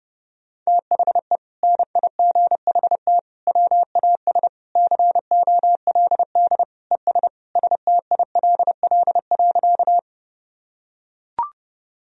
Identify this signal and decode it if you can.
Morse code, heard as a keyed tone, about 32 wpm, 700 Hz. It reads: THE NIGHT WAS COLD ES STILL.
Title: Sentences from Top 400 Words and Abbreviations 30wpm